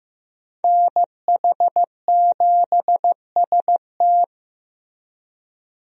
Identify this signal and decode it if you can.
Morse code, heard as a keyed tone, about 15 wpm, 700 Hz. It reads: NH7ST
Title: USA Call Signs - 5 Characters - Repeated 3x - 15wpm